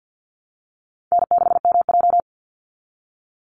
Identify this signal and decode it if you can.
Morse code, heard as a keyed tone, about 48 wpm, 700 Hz. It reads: N6MJ
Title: Easy Call Signs Repeated 3x - 50wpm